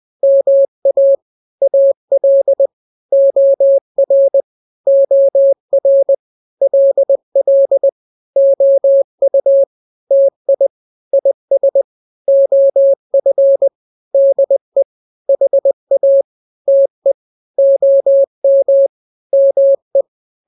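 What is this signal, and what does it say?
Morse code, heard as a keyed tone, about 20 wpm, 550 Hz. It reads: MA AL OR OR LL OU TI IS OF DE HA TE OM ME
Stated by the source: Morse drill of two-letter n-grams at 20/18 WPM (Farnsworth), from KK7RR.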